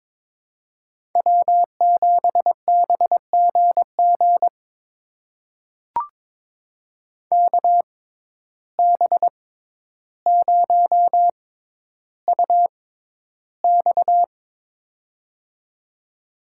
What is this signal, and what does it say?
Morse code, heard as a keyed tone, about 22 wpm, spaced as for 11 wpm, 700 Hz. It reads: W7BGG KB0UX
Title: USA Call Signs - 5 Characters - 6x Character Spacing - 22wpm